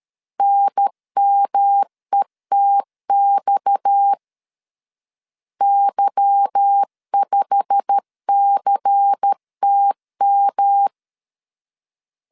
Morse code, heard as a keyed tone, about 13 wpm, 800 Hz. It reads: NMETX Y5CTM
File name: LEZIONI / 10 SIGNS / 10A SLASH 12wpm 2314